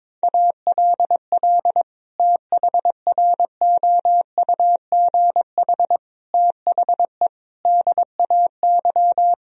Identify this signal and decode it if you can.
Morse code, heard as a keyed tone, about 22 wpm, 700 Hz. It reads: ALL THROUGH THE DAY
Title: Sentences from Top 1000 Words - Review Part 1 - 22wpm